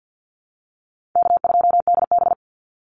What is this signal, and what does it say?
Morse code, heard as a keyed tone, about 50 wpm, 700 Hz. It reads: K2DB
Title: Easy Call Signs - 50wpm